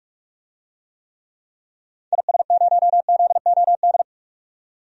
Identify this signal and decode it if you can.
Morse code, heard as a keyed tone, about 42 wpm, 700 Hz. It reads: IS0ZOD
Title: DXCC-100 Call Signs - 6 Characters - 45wpm